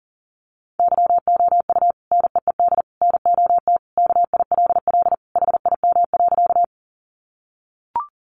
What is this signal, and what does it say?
Morse code, heard as a keyed tone, about 40 wpm, 700 Hz. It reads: YOU NEED NOT KILL HIM.
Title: Sentences from Top 1400 Words 40wpm